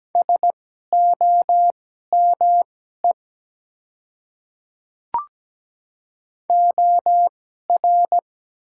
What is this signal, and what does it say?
Morse code, heard as a keyed tone, about 17 wpm, spaced as for 12 wpm, 700 Hz. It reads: SOME OR
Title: 2x Character Spacing - Top 100 Words - 17wpm